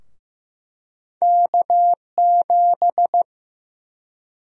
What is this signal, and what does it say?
Morse code, heard as a keyed tone, about 15 wpm, 700 Hz. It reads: K7
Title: USA Call Sign Prefixes - 15wpm